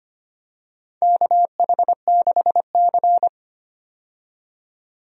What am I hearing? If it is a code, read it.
Morse code, heard as a keyed tone, about 25 wpm, 700 Hz. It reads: KH6C
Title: DXCC-100 Call Signs - 4 Characters - 25wpm